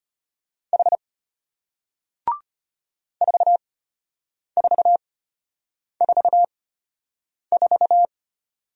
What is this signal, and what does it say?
Morse code, heard as a keyed tone, about 33 wpm, spaced as for 8 wpm, 700 Hz. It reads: H 4444